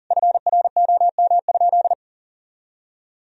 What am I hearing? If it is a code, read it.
Morse code, heard as a keyed tone, about 40 wpm, 700 Hz. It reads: FROM?
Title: Sentences from Top 1000 Words - Review Part 1 - 40wpm